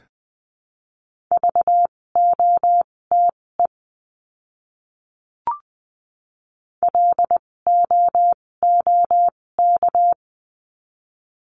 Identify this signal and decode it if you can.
Morse code, heard as a keyed tone, about 20 wpm, spaced as for 16 wpm, 700 Hz. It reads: VOTE LOOK
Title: 256 Lesson 42a - Words 12-20wpm